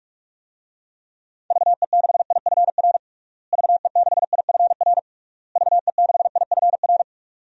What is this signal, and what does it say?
Morse code, heard as a keyed tone, about 43 wpm, 700 Hz. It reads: VE6IFR VE6IFR VE6IFR
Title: DXCC-100 Call Signs - 6 Characters - Repeated 3x - 45wpm